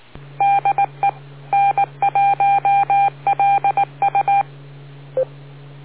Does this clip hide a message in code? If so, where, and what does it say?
Yes: Morse code from 0.4 to 4.4 s, heard as a keyed tone, about 19 wpm, 800 Hz. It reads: DE N1LU